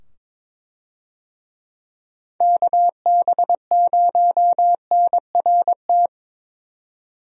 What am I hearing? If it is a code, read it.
Morse code, heard as a keyed tone, about 21 wpm, 700 Hz. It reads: KB0NRT
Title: USA Call Signs - 6 Characters - 22wpm